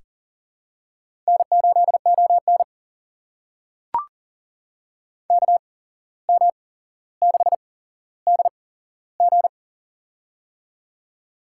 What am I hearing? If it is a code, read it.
Morse code, heard as a keyed tone, about 40 wpm, spaced as for 16 wpm, 700 Hz. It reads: N8ON KM6DG
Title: USA Call Signs - 8x Character Spacing - 40wpm